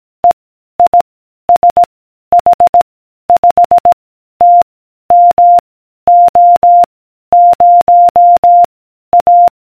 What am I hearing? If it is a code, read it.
Morse code, heard as a keyed tone, about 18 wpm, 700 Hz. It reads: E I S H 5 T M O 0 A